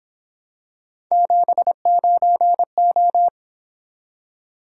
Morse code, heard as a keyed tone, about 26 wpm, 700 Hz. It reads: 79O
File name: Speed-Racing - Three Letter-Number Combinations - 22wpm